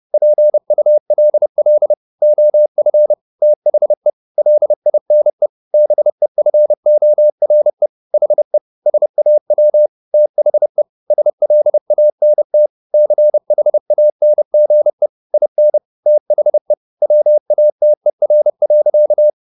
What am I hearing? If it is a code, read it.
Morse code, heard as a keyed tone, about 30 wpm, 600 Hz. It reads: PULL OF THE LINE BEFORE HE SAW THE SLANT CHANGE IN THE WATER.
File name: the old man and the sea / the old man and the sea 1 30wpm